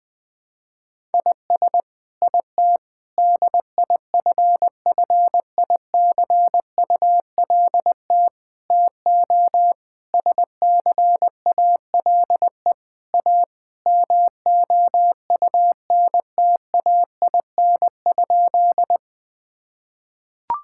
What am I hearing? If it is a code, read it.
Morse code, heard as a keyed tone, about 21 wpm, 700 Hz. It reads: IS IT DIFFICULT TO SCALE A MOUNTAIN?
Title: Sentences from Top 1000 Words - Review Part 5 - 20wpm